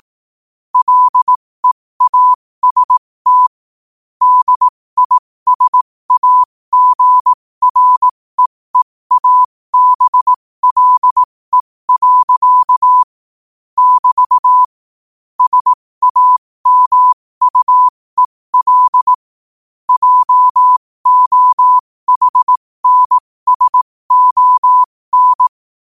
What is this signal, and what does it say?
Morse code, heard as a keyed tone, about 18 wpm, spaced as for 15 wpm, 1000 Hz. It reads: LEAST DISAGREEABLE. <BT> SAMUEL JOHNSON
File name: qotdfiles / 15wpm